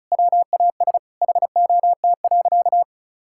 Morse code, heard as a keyed tone, about 36 wpm, 700 Hz. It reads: WAS HOT.
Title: Sentences from Top 1000 Words - Review Part 5 - 35wpm